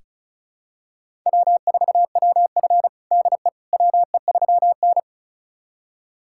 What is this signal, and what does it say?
Morse code, heard as a keyed tone, about 36 wpm, 700 Hz. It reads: W4WF DE WE3N
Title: USA Callsign DE Callsign - 4 Characters - 35wpm